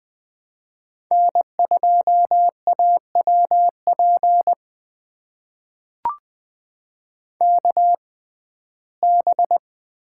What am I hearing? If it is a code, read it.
Morse code, heard as a keyed tone, about 20 wpm, spaced as for 14 wpm, 700 Hz. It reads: N2AWP KB
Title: USA Call Signs - 6x Character Spacing - 20wpm